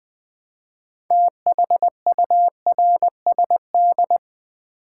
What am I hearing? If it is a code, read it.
Morse code, heard as a keyed tone, about 20 wpm, 700 Hz. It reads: THURSD
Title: Build A Word - 2 Letters At A Time - 8-Letter Words - 20wpm